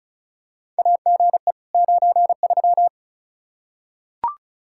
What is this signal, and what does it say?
Morse code, heard as a keyed tone, about 34 wpm, 700 Hz. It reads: AGE 93
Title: K1USN SST QSO Practice 35wpm